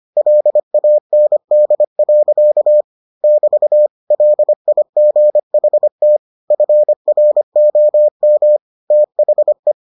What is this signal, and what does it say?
Morse code, heard as a keyed tone, about 25 wpm, 600 Hz. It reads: LAND. <BT> LIGHT FROM THE